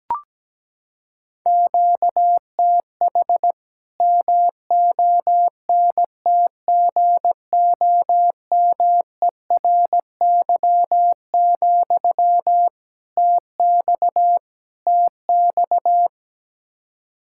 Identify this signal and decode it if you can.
Morse code, heard as a keyed tone, about 17 wpm, 700 Hz. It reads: QTH MONTGOMERY, TX TX
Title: Practice QSO - Exchange 1 - Beginning - 17wpm